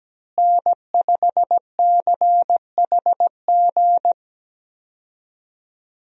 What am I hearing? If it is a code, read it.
Morse code, heard as a keyed tone, about 17 wpm, 700 Hz. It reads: N5CHG
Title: USA Call Signs - 5 Characters - Repeated 2x - 17wpm